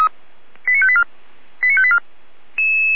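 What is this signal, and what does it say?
Five-tone selective call (ZVEI), heard as quick runs of five tones, tones of about 70 ms. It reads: 89584 89584
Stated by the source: ZVEI five-tone call received via SDR.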